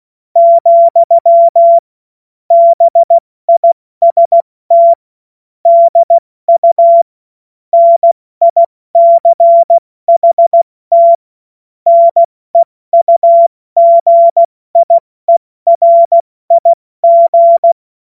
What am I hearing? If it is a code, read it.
Morse code, heard as a keyed tone, about 16 wpm, 680 Hz. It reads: , BIST DU NICHT NEUGIERIG
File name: Morsekurs / Witze / Witz 25. April 70 F80